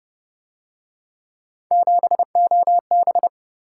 Morse code, heard as a keyed tone, about 30 wpm, 700 Hz. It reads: 7OB